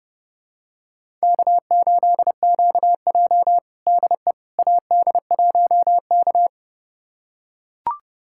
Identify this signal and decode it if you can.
Morse code, heard as a keyed tone, about 32 wpm, 700 Hz. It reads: K8QJ DE AD1K
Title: USA Callsign DE Callsign - 4 Characters - 30wpm